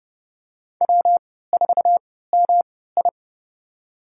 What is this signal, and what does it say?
Morse code, heard as a keyed tone, about 30 wpm, spaced as for 17 wpm, 700 Hz. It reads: W4MI
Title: USA Call Signs - 4 Characters - 3x Character Spacing - 30wpm